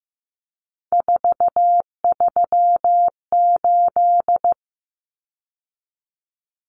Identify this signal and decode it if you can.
Morse code, heard as a keyed tone, about 15 wpm, 700 Hz. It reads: 438
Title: Common Cut Numbers - 3-Digit 15wpm